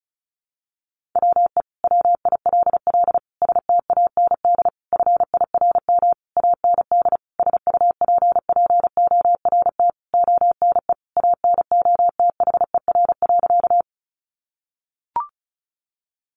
Morse code, heard as a keyed tone, about 34 wpm, 700 Hz. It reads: WE WILL STAND FIRM AND SUPPORT ONE ANOTHER.
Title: Sentences from Top 1400 Words 35wpm